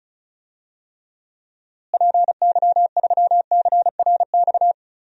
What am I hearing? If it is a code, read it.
Morse code, heard as a keyed tone, about 34 wpm, 700 Hz. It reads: PY3CRX
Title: DXCC-100 Call Signs - 6 Characters - Repeated 3x - 35wpm